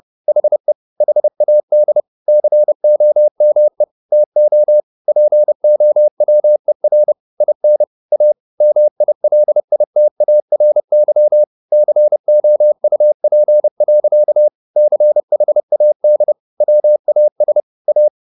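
Morse code, heard as a keyed tone, about 30 wpm, 600 Hz. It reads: HE HAD COME TO POWER IN A MILITARY COUP. CHAD WAS A
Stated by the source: Morse code at 30 wpm, the novel Never by Ken Follett in English, from CW med Gnister.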